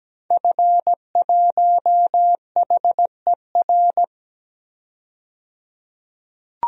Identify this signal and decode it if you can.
Morse code, heard as a keyed tone, about 17 wpm, 700 Hz. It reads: F1HER
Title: DXCC-100 Call Signs - 5 Characters - 17wpm